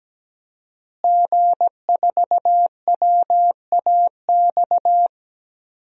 Morse code, heard as a keyed tone, about 17 wpm, 700 Hz. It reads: G4WAX
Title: DXCC-100 Call Signs - 5 Characters - Repeated 3x - 17wpm